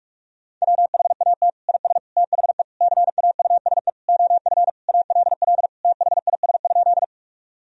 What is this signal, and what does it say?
Morse code, heard as a keyed tone, about 46 wpm, 700 Hz. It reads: WHAT IS THE CAUSE OF ALL THIS?
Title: Sentences from Top 1000 Words - Review Part 1 - 45wpm